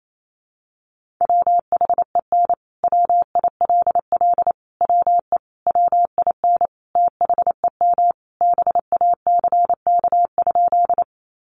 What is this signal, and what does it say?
Morse code, heard as a keyed tone, about 28 wpm, 700 Hz. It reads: WHEN WILL WE WIN THEM BACK?
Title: Sentences from Top 1500 Words 28wpm